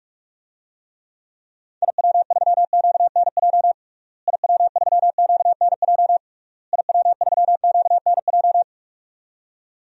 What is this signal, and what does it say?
Morse code, heard as a keyed tone, about 45 wpm, 700 Hz. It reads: IW3QNJ IW3QNJ IW3QNJ
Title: DXCC-100 Call Signs - 6 Characters - Repeated 3x - 45wpm